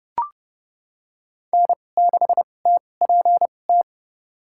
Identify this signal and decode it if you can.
Morse code, heard as a keyed tone, about 30 wpm, spaced as for 22 wpm, 700 Hz. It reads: N6TPT
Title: USA Call Signs - 5 Characters - 2x Character Spacing - 30wpm